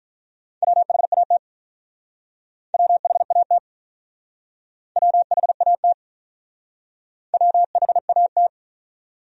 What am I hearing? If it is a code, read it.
Morse code, heard as a keyed tone, about 47 wpm, 700 Hz. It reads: WHAT WHAT WHAT WHAT